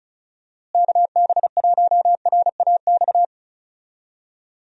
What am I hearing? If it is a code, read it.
Morse code, heard as a keyed tone, about 36 wpm, 700 Hz. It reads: KB1RAX
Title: USA Call Signs - 6 Characters - Repeated 2x - 35wpm